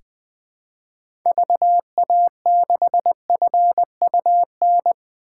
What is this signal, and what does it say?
Morse code, heard as a keyed tone, about 20 wpm, 700 Hz. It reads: VA6FUN
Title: 2023 SOTA Chaser Call Signs - 20wpm